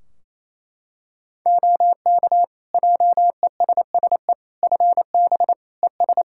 Morse code, heard as a keyed tone, about 27 wpm, 700 Hz. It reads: OK JESSE FB ES
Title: Practice QSO - Exchange 2 - Info - 28wpm